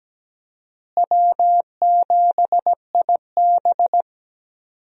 Morse code, heard as a keyed tone, about 17 wpm, 700 Hz. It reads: W7IB
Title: USA Call Signs - 4 Characters - Repeated 2x - 17wpm